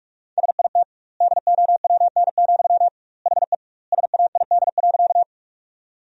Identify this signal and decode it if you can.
Morse code, heard as a keyed tone, about 43 wpm, 700 Hz. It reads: SIT DOWN, HE SAID.
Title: 2x Word-Spacing - Sentences from Top 500 Words - 45wpm